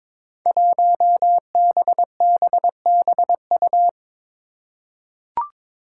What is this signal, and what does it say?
Morse code, heard as a keyed tone, about 22 wpm, 700 Hz. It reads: 1BBBU
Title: Five Challenging Letter-Number Combinations 22wpm